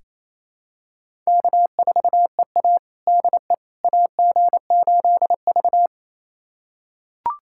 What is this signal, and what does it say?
Morse code, heard as a keyed tone, about 28 wpm, 700 Hz. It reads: K4EA DE AG8V
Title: USA Callsign DE Callsign - 4 Characters - 28wpm